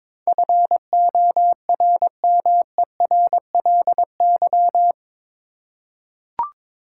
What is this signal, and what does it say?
Morse code, heard as a keyed tone, about 22 wpm, 700 Hz. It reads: FORMERLY
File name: practice / 22wpm / Eight Letter Words - 22wpm